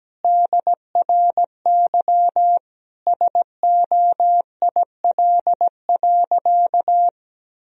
Morse code, heard as a keyed tone, about 17 wpm, 700 Hz. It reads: DRY SOIL.
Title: Sentences from Top 1000 Words - Review Part 4 - 17wpm